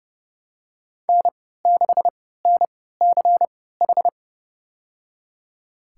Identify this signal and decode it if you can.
Morse code, heard as a keyed tone, about 30 wpm, spaced as for 17 wpm, 700 Hz. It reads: N6NCH
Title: USA Call Signs - 5 Characters - 3x Character Spacing - 30wpm